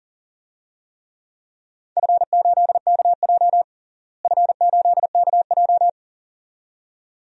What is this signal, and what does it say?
Morse code, heard as a keyed tone, about 37 wpm, 700 Hz. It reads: F8KJ F8KJ
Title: DXCC-100 Call Signs - 4 Characters - Repeated 2x - 40wpm